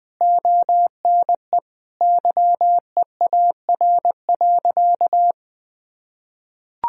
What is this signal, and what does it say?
Morse code, heard as a keyed tone, about 20 wpm, 700 Hz. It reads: ONE YEAR.
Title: Sentences from Top 1000 Words - Review Part 1 - 20wpm